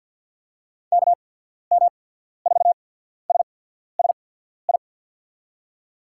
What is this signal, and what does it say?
Morse code, heard as a keyed tone, about 47 wpm, spaced as for 14 wpm, 700 Hz. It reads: KM4SSI
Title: USA Call Signs - 6 Characters - 8x Character Spacing - 50wpm